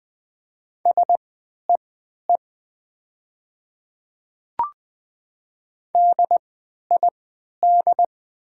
Morse code, heard as a keyed tone, about 20 wpm, spaced as for 11 wpm, 700 Hz. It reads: SEE DID
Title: 3x Character Spacing - Top 300 Words - 20wpm